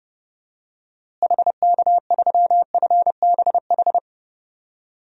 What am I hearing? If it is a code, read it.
Morse code, heard as a keyed tone, about 32 wpm, 700 Hz. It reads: HK3FBH
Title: DXCC-100 Call Signs - 6 Characters - Repeated 2x - 30wpm